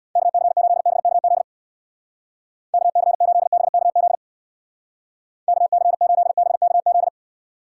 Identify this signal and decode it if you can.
Morse code, heard as a keyed tone, about 62 wpm, 700 Hz. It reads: KC8BKZ KC8BKZ KC8BKZ
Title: Speed Racing - USA Call Signs 50wpm